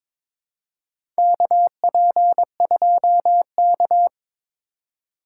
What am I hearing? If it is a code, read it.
Morse code, heard as a keyed tone, about 21 wpm, 700 Hz. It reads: KP2K
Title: USA Call Signs - 4 Characters - 3x Character Spacing - 22wpm